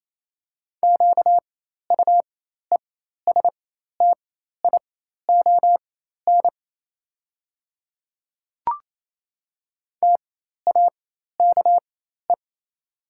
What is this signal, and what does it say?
Morse code, heard as a keyed tone, about 28 wpm, spaced as for 13 wpm, 700 Hz. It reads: QUESTION TAKE
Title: 4x Character Spacing - Top 500 Words - 28wpm